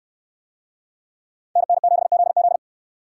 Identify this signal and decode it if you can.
Morse code, heard as a keyed tone, about 69 wpm, 700 Hz. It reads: NN7ZZ